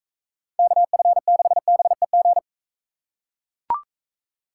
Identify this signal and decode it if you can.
Morse code, heard as a keyed tone, about 43 wpm, 700 Hz. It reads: KF6BEG